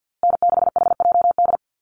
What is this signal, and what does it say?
Morse code, heard as a keyed tone, about 50 wpm, 700 Hz. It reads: N6HJD
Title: USA Call Signs Repeated 3x 50wpm